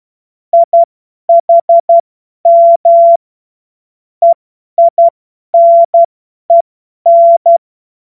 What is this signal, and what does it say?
Morse code, heard as a keyed tone, about 12 wpm, spaced as for 10 wpm, 680 Hz. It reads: IHM EINEN
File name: Morsekurs / Witze / Witz 07. Juni 2022 50 F70